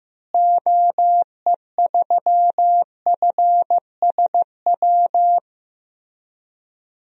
Morse code, heard as a keyed tone, about 15 wpm, 700 Hz. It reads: OE3FSW
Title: DXCC-100 Call Signs - 6 Characters - Repeated 3x - 15wpm